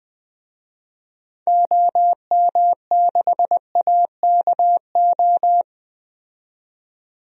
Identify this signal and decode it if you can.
Morse code, heard as a keyed tone, about 20 wpm, 700 Hz. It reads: OM6AKO
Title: DXCC-100 Call Signs - 6 Characters - 20wpm